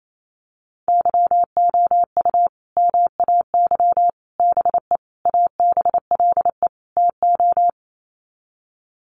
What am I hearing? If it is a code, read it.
Morse code, heard as a keyed tone, about 28 wpm, 700 Hz. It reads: YOU MAY BE ABLE TO
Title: Sets of 5 Words Encore 28wpm